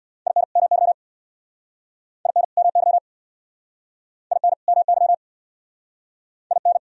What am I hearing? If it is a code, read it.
Morse code, heard as a keyed tone, about 67 wpm, 700 Hz. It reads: IN MY IN MY IN MY IN